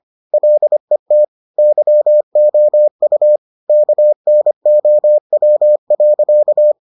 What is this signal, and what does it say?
Morse code, heard as a keyed tone, about 25 wpm, 600 Hz. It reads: LET YOU KNOW.